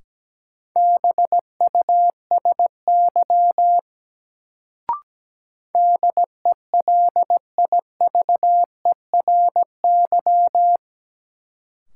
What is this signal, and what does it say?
Morse code, heard as a keyed tone, about 17 wpm, 700 Hz. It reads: BUSY DELIVERY
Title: Sentences from Top 1500 Words - New Word Drill Alt - 17wpm